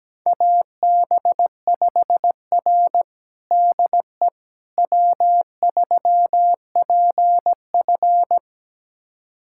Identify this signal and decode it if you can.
Morse code, heard as a keyed tone, about 17 wpm, 700 Hz. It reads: AB5R DE W3PF
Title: USA Callsign DE Callsign - 4 Characters - 17wpm